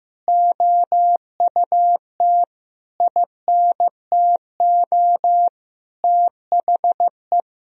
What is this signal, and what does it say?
Morse code, heard as a keyed tone, about 15 wpm, 700 Hz. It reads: OUT INTO THE
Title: Sentences from Top 1000 Words - Review Part 4 - 15wpm